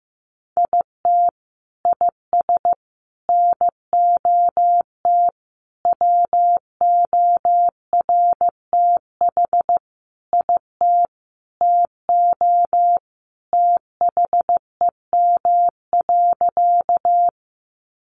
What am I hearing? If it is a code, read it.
Morse code, heard as a keyed tone, about 15 wpm, 700 Hz. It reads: IT IS NOT WORTH IT TO THEM.
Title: Sentences from Top 800 Words 15wpm